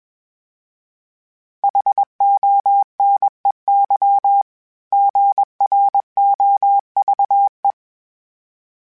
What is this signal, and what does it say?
Morse code, heard as a keyed tone, about 21 wpm, 800 Hz. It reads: HONEY GROVE
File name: fromSMs / FromSMs mp3s / Cities@20